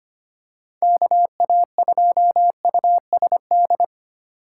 Morse code, heard as a keyed tone, about 29 wpm, 700 Hz. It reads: KA2USD
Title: Speed Racing - USA 2x3 Call Signs 25wpm